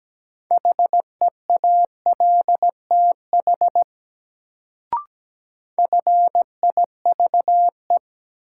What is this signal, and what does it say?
Morse code, heard as a keyed tone, about 17 wpm, 700 Hz. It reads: HEALTH FIVE
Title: Sets of 5 Words - New Word Drill - 17wpm